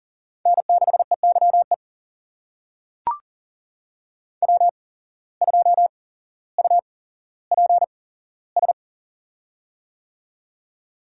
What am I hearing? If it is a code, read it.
Morse code, heard as a keyed tone, about 38 wpm, spaced as for 17 wpm, 700 Hz. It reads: N6EYE W2UPS
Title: USA Call Signs - 5 Characters - 8x Character Spacing - 40wpm